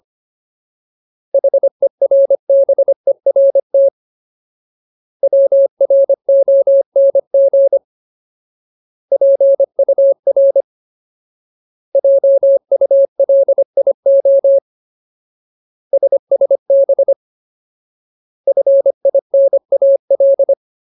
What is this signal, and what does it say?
Morse code, heard as a keyed tone, about 25 wpm, 550 Hz. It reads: HERBERT WRONG PUR JULIO SSB FINAL
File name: Ultimate Word List / 25WPM Ultimate